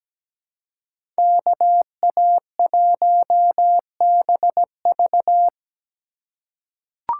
Morse code, heard as a keyed tone, about 17 wpm, 700 Hz. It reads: KA1BV
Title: USA Call Signs - 5 Characters - 2x Character Spacing - 17wpm